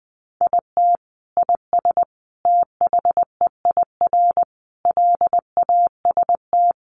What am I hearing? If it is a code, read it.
Morse code, heard as a keyed tone, about 19 wpm, 700 Hz. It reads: IT IS THEIR LAST
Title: Sentences from Top 1300 Words 20wpm